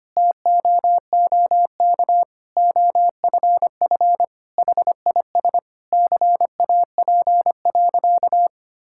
Morse code, heard as a keyed tone, about 25 wpm, 700 Hz. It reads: TOOK OFF HIS CAP.
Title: Sentences from Top 1600 Words 25wpm